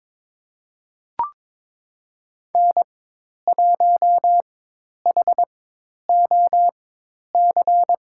Morse code, heard as a keyed tone, about 22 wpm, spaced as for 10 wpm, 700 Hz. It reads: N1HOC
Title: USA Call Signs - 5 Characters - 4x Character Spacing - 22wpm